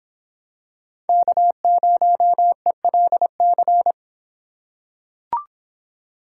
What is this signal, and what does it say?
Morse code, heard as a keyed tone, about 26 wpm, 700 Hz. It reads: K0ELC